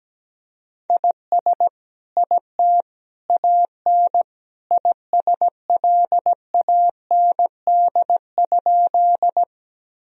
Morse code, heard as a keyed tone, about 17 wpm, 700 Hz. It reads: IS IT AN ISLAND?
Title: Sentences from Top 1000 Words - Review Part 3 - 17wpm